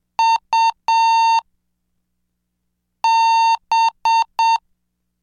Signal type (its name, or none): Morse code